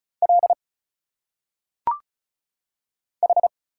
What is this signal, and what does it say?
Morse code, heard as a keyed tone, about 36 wpm, 700 Hz. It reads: L H
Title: Single Letters - Phonetic Pronunciation 35wpm